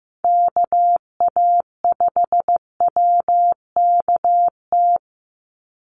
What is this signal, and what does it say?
Morse code, heard as a keyed tone, about 15 wpm, 700 Hz. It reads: KA5WKT
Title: USA Call Signs 15wpm